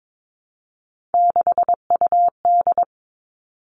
Morse code, heard as a keyed tone, about 22 wpm, 700 Hz. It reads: 6UD
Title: All Cut Numbers - 3-Digit 22wpm